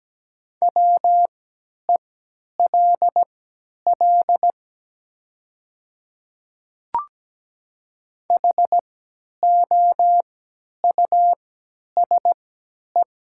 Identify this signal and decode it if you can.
Morse code, heard as a keyed tone, about 17 wpm, spaced as for 10 wpm, 700 Hz. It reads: WELL HOUSE
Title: 3x Character Spacing - Top 300 Words - 17wpm